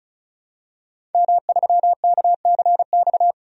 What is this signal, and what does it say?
Morse code, heard as a keyed tone, about 36 wpm, 700 Hz. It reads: M3KCX
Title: DXCC-100 Call Signs - 5 Characters - 35wpm